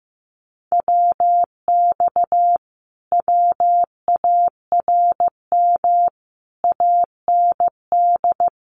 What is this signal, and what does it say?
Morse code, heard as a keyed tone, about 15 wpm, 700 Hz. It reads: WX WARM AND
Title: Common QSO Phrases Encore 15wpm